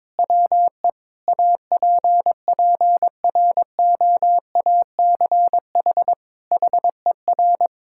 Morse code, heard as a keyed tone, about 22 wpm, 700 Hz. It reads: WE APPROACH HER
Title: Sentences from Top 1000 Words - Review Part 6 - 22wpm